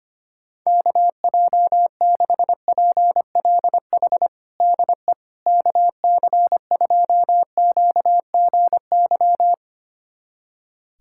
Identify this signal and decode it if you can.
Morse code, heard as a keyed tone, about 25 wpm, 700 Hz. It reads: KJ6PLH DE KC2QGY
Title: USA Callsign DE Callsign - 6 Characters - 25wpm